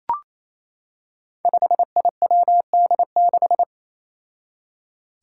Morse code, heard as a keyed tone, about 27 wpm, 700 Hz. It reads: 5IWD6
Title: Five Challenging Letter-Number Combinations 28wpm